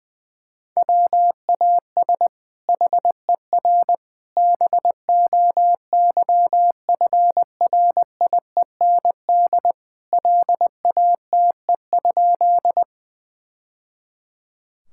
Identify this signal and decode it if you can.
Morse code, heard as a keyed tone, about 20 wpm, 700 Hz. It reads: WAS HER BOYFRIEND LATE?
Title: Sentences from Top 1000 Words - Review Part 5 - 20wpm